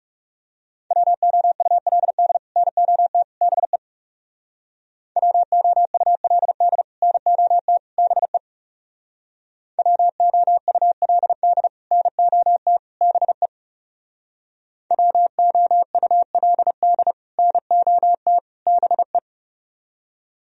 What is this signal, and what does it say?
Morse code, heard as a keyed tone, about 36 wpm, 700 Hz. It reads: WOULD NOT BE WOULD NOT BE WOULD NOT BE WOULD NOT BE
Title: Speed-Racing - Sets of 3 Words 30wpm